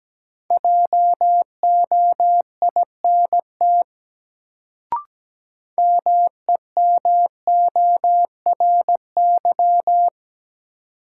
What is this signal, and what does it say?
Morse code, heard as a keyed tone, about 17 wpm, 700 Hz. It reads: JOINT MEMORY
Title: Sentences from Top 1300 Words - New Word Drill Alt - 17wpm